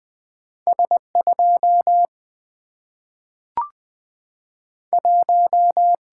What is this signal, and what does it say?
Morse code, heard as a keyed tone, about 20 wpm, 700 Hz. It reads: S2 1
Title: Single-Letter Single-Number 20wpm